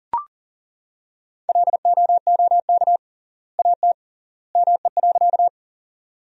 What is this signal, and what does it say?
Morse code, heard as a keyed tone, about 40 wpm, 700 Hz. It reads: LOOK AT ME.
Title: 3x Word-Spacing - Sentences from Top 200 Words - 40wpm